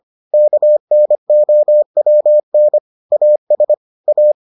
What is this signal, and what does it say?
Morse code, heard as a keyed tone, about 25 wpm, 600 Hz. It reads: KNOWN AS A